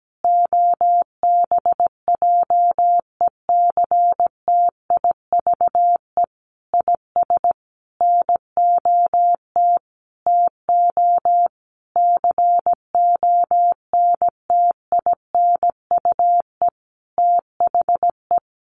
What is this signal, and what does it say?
Morse code, heard as a keyed tone, about 17 wpm, 700 Hz. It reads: OBJECTIVE IS NOT TO CONTINUE THE
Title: Sentences from Top 1200 Words 17wpm